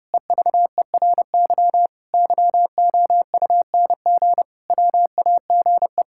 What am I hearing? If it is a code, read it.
Morse code, heard as a keyed tone, about 30 wpm, 700 Hz. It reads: EVERY YOUNG WAGE
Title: Pangram Challenge v2 - 30wpm